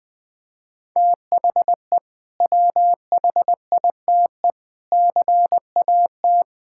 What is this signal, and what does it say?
Morse code, heard as a keyed tone, about 21 wpm, 700 Hz. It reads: THE WHITE CAT
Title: Sentences with 2-Word Phrases - 20wpm